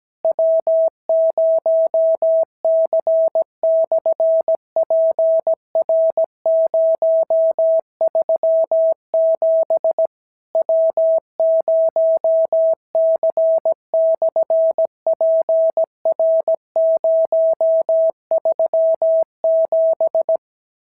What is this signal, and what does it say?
Morse code, heard as a keyed tone, about 17 wpm, 650 Hz. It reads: W0C/PR037 W0C/PR037
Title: SOTA QSO Practice 17wpm